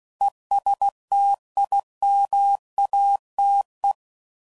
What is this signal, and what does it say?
Morse code, heard as a keyed tone, about 16 wpm, 800 Hz. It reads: ESTIMATE